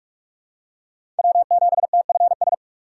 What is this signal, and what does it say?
Morse code, heard as a keyed tone, about 46 wpm, 700 Hz. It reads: W7TFS
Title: USA Call Signs - 6x Character Spacing - 45wpm